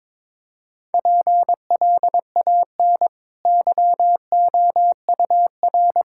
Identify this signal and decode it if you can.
Morse code, heard as a keyed tone, about 22 wpm, 690 Hz. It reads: PLAN YOUR
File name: practice / 22wpm / Easy Sentences with Hints - 22wpm